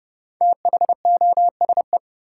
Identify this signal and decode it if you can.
Morse code, heard as a keyed tone, about 30 wpm, 700 Hz. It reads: THOSE
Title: Sentences from Top 300 Words - New Word Drill - 30wpm